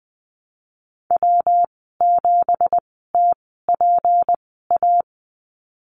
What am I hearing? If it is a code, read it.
Morse code, heard as a keyed tone, about 20 wpm, spaced as for 14 wpm, 700 Hz. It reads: W7TPA
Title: 231 Lesson 37g - Letter x - Call Signs 20wpm